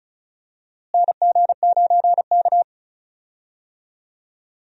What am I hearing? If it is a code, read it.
Morse code, heard as a keyed tone, about 36 wpm, 700 Hz. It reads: NG9K
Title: USA Call Signs - 4 Characters - Repeated 2x - 35wpm